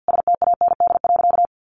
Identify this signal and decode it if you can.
Morse code, heard as a keyed tone, about 49 wpm, 700 Hz. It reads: STAND.
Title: Sentences from Top 1300 Words 50wpm